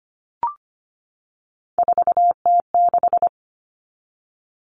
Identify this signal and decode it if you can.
Morse code, heard as a keyed tone, about 25 wpm, 700 Hz. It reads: 4T6
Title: Common Cut Numbers - 3-Digit 25wpm